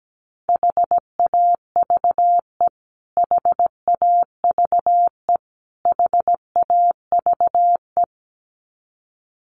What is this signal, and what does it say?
Morse code, heard as a keyed tone, about 17 wpm, 700 Hz. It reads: HAVE HAVE HAVE
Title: Top 500 Words Repeated 17wpm